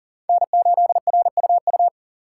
Morse code, heard as a keyed tone, about 40 wpm, 700 Hz. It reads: N8RUU
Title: USA Call Signs - 5 Characters - 6x Character Spacing - 40wpm